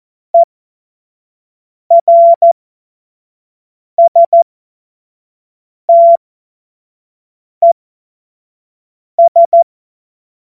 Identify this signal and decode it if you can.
Morse code, heard as a keyed tone, about 14 wpm, spaced as for 5 wpm, 680 Hz. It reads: ERSTES
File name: Morsekurs / Witze / Witz 24.Oktober 2023 25 F70